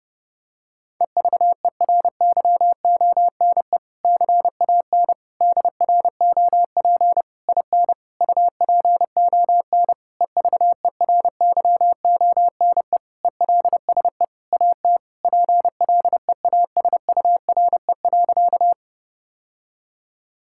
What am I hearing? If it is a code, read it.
Morse code, heard as a keyed tone, about 30 wpm, 700 Hz. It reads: EVERYONE CAN DROP IN UPON EVERYONE ELSE AT PLEASURE.
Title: Sentences from Top 1600 Words 30wpm